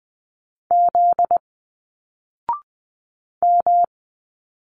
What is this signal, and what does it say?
Morse code, heard as a keyed tone, about 20 wpm, 700 Hz. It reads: Z M